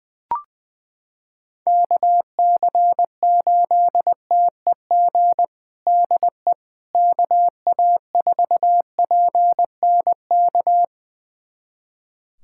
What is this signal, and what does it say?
Morse code, heard as a keyed tone, about 20 wpm, 700 Hz. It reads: KC8TEG DE KA4PNK
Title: USA Callsign DE Callsign 20wpm